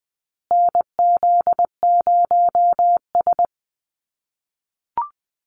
Morse code, heard as a keyed tone, about 20 wpm, 700 Hz. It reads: NZ0S
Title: USA Call Signs Repeated 3x 20wpm